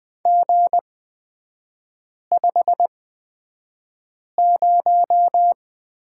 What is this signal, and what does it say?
Morse code, heard as a keyed tone, about 20 wpm, spaced as for 5 wpm, 700 Hz. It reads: G50
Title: 243 Lesson 39e - Letter z - Three Characters 20wpm